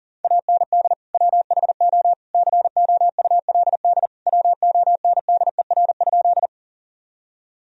Morse code, heard as a keyed tone, about 40 wpm, 700 Hz. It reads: AND WHO COULD WONDER?
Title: Sentences from Top 1000 Words - Review Part 3 - 40wpm